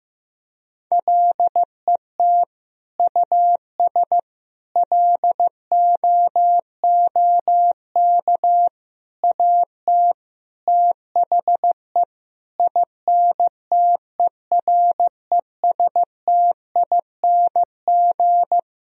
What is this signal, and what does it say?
Morse code, heard as a keyed tone, about 15 wpm, 700 Hz. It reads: LET US LOOK AT THE INTERESTING
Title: Sentences from Top 1000 Words - Review Part 2 - 15wpm